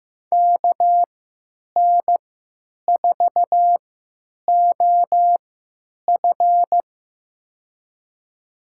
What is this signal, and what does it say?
Morse code, heard as a keyed tone, about 15 wpm, spaced as for 9 wpm, 700 Hz. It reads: KN4OF
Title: USA Call Signs - 3x Character Spacing - 15wpm